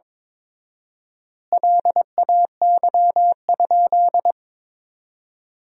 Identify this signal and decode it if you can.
Morse code, heard as a keyed tone, about 21 wpm, 700 Hz. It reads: LAY?
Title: 4x Word-Spacing - Sentences from Top 400 Words - 22wpm